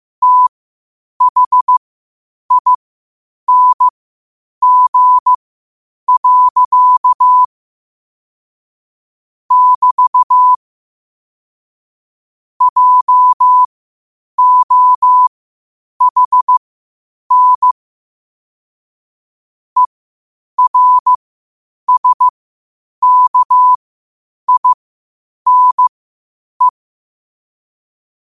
Morse code, heard as a keyed tone, about 15 wpm, spaced as for 8 wpm, 1000 Hz. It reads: THING. <BT> JOHN ERSKINE
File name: qotdfiles / 8wpm